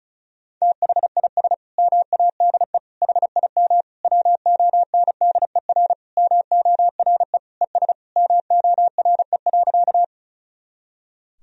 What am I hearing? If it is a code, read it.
Morse code, heard as a keyed tone, about 36 wpm, 700 Hz. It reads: THIS MADE HIM WONDER MORE ES MORE.
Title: Sentences from Top 500 Words and Abbreviations 35wpm